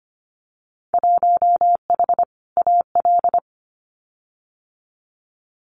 Morse code, heard as a keyed tone, about 25 wpm, 700 Hz. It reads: 1H AL
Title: Winter Field Day Exchanges 25wpm